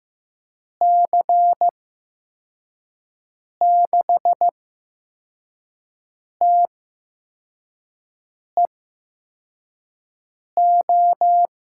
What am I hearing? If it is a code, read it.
Morse code, heard as a keyed tone, about 15 wpm, spaced as for 4 wpm, 700 Hz. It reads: C6TEO
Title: USA Call Signs - 6 Characters - 8x Character Spacing - 15wpm